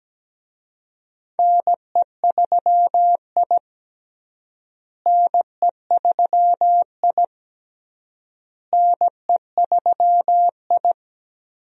Morse code, heard as a keyed tone, about 17 wpm, 700 Hz. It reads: NE3I NE3I NE3I
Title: USA Call Signs - 4 Characters - Repeated 3x - 17wpm